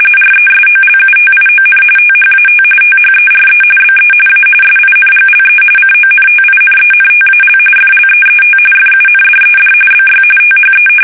RTTY, about 45.45 baud. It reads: RESOLUTION THAT THERE BE A SINGLE CONTINENTAL FLEET FUNDED BY THE
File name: fsk850 / usnavy1